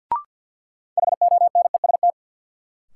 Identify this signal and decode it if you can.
Morse code, heard as a keyed tone, about 49 wpm, 700 Hz. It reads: HONEST
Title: Sets of 5 Words Encore - New Word Drill - 50wpm